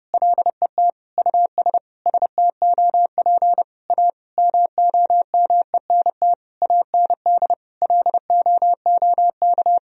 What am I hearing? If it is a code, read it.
Morse code, heard as a keyed tone, about 30 wpm, 700 Hz. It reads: LET US STOP A MOMENT AND LOOK
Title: Sentences from Top 1000 Words - Review Part 1 - 30wpm